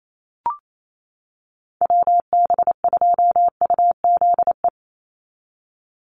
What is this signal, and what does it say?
Morse code, heard as a keyed tone, about 28 wpm, 700 Hz. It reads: WB2UZE
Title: Call Signs - 28wpm